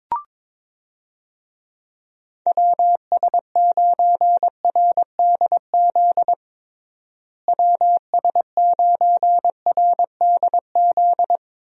Morse code, heard as a keyed tone, about 22 wpm, 700 Hz. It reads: WS9RDZ WS9RDZ
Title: USA Call Signs - 6 Characters - Repeated 2x - 22wpm